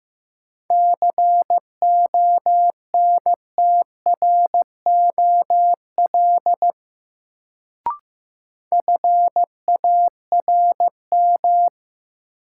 Morse code, heard as a keyed tone, about 15 wpm, 700 Hz. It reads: CONTROL FARM
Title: Sentences from Top 1000 Words - Review Part 2 - New Word Drill - 15wpm